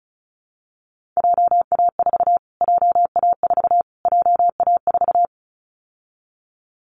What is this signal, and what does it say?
Morse code, heard as a keyed tone, about 34 wpm, 700 Hz. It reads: JA4 JA4 JA4
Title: Callsign Prefixes for Top 25 DXCC - Repeated 3x - 35wpm